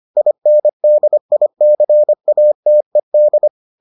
Morse code, heard as a keyed tone, about 25 wpm, 600 Hz. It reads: INDICATED